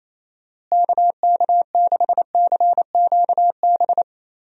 Morse code, heard as a keyed tone, about 28 wpm, 700 Hz. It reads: KK6CQB
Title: USA Call Signs - 6 Characters - Repeated 2x - 28wpm